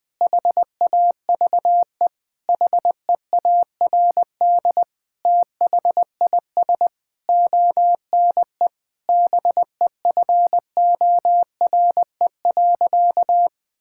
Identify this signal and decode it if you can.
Morse code, heard as a keyed tone, about 21 wpm, 700 Hz. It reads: HAVE HEARD THIS ONE BEFORE.
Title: Sentences with 8-Word Phrases - 20wpm